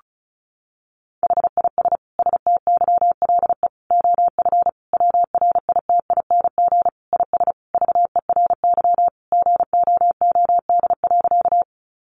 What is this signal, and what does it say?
Morse code, heard as a keyed tone, about 34 wpm, 700 Hz. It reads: HIS STYLE OF WRITING IS VERY GOOD.
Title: Sentences from Top 800 Words 35wpm